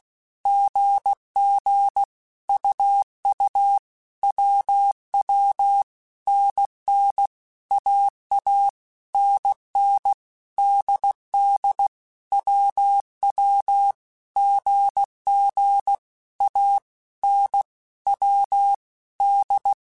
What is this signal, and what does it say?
Morse code, heard as a keyed tone, about 16 wpm, 800 Hz. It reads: GG UU WW NN AA NN DD WW GG A N W D
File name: Basic Letter Groups By7Groups / AWUNDG / AWUNDG@15